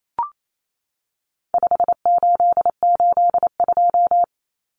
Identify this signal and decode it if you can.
Morse code, heard as a keyed tone, about 28 wpm, 700 Hz. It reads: 5882
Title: Numbers - 4-Digit 28wpm